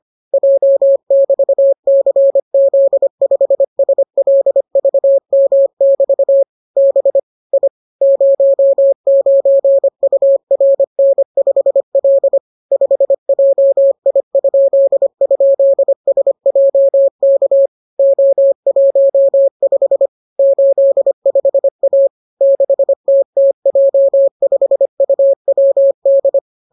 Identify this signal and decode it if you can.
Morse code, heard as a keyed tone, about 25 wpm, 550 Hz. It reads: J<BT>CZ5SLVM<BT> B I 09URN5L 5JI??SJK O15 85A 6TTJ5UWD